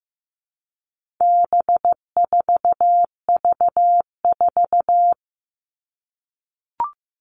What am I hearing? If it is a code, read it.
Morse code, heard as a keyed tone, about 15 wpm, 700 Hz. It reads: B4V4
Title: All Cut Numbers - 4-Digit 15wpm